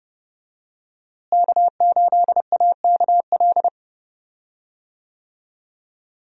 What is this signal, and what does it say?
Morse code, heard as a keyed tone, about 30 wpm, 700 Hz. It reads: K8AKL
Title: USA Call Signs - 5 Characters - Repeated 2x - 30wpm